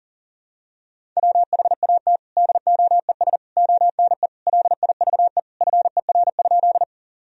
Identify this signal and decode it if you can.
Morse code, heard as a keyed tone, about 40 wpm, 700 Hz. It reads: WHAT DOES ONE LIVE FER?
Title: Sentences from Top 200 Words and Abbreviations 40wpm